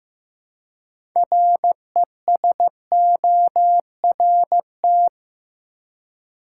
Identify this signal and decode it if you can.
Morse code, heard as a keyed tone, about 15 wpm, 700 Hz. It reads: RESORT
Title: Six Letter Words - 15wpm